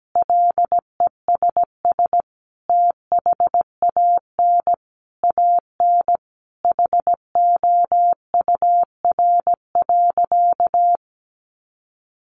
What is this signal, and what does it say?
Morse code, heard as a keyed tone, about 17 wpm, 700 Hz. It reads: LESS THAN AN HOUR.
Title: Sentences from Top 500 Words 17wpm